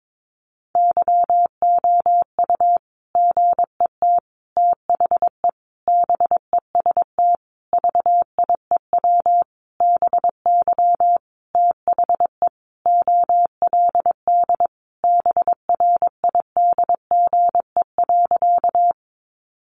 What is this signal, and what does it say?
Morse code, heard as a keyed tone, about 22 wpm, 700 Hz. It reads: YOU GET THE BEST VIEW BY THE OLD BRIDGE.
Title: Sentences from Top 1200 Words 22wpm